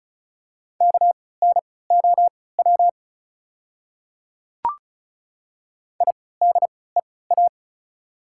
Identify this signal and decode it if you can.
Morse code, heard as a keyed tone, about 36 wpm, spaced as for 20 wpm, 700 Hz. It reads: KNOW IDEA
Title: 3x Character Spacing - Top 300 Words - 35wpm